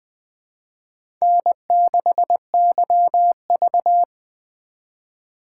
Morse code, heard as a keyed tone, about 20 wpm, 700 Hz. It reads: N6YV
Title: USA Call Signs - 4 Characters - 3x Character Spacing - 20wpm